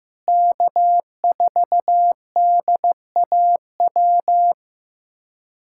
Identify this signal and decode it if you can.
Morse code, heard as a keyed tone, about 15 wpm, 700 Hz. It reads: K4DAW
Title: USA Callsign DE Callsign 15wpm